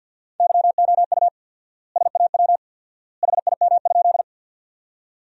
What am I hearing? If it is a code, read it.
Morse code, heard as a keyed tone, about 53 wpm, 700 Hz. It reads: YOU SAW HIM?
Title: 4x Word-Spacing - Sentences from Top 300 Words - 50wpm